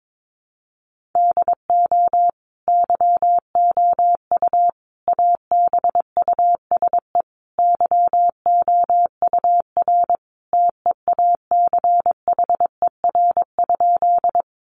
Morse code, heard as a keyed tone, about 22 wpm, 700 Hz. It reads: DO YOU ABUSE YOUR TEACHER?
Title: Sentences from Top 1300 Words 22wpm